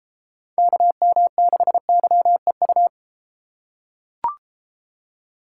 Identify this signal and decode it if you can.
Morse code, heard as a keyed tone, about 32 wpm, 700 Hz. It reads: KM6YEU